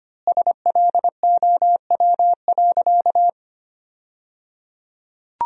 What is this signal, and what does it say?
Morse code, heard as a keyed tone, about 25 wpm, 700 Hz. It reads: SLOW.
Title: 3x Word-Spacing - Sentences from Top 400 Words - 25wpm